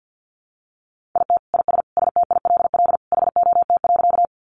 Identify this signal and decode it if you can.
Morse code, heard as a keyed tone, about 50 wpm, 700 Hz. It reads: IT IS STILL HOT.
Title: Additional Easy Phrases 50wpm